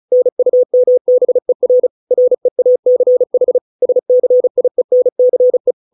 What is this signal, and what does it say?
Morse code, heard as a keyed tone, about 34 wpm, 500 Hz. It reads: NUMBER REACH SCIENCE